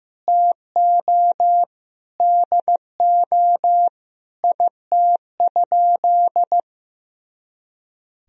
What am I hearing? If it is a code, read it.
Morse code, heard as a keyed tone, about 15 wpm, 700 Hz. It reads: TO DO IT?
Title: Sentences from Top 200 Words and Abbreviations 15wpm